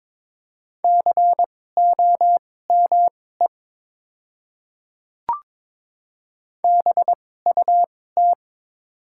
Morse code, heard as a keyed tone, about 22 wpm, spaced as for 16 wpm, 700 Hz. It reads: COME BUT
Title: 2x Character Spacing - Top 100 Words - 22wpm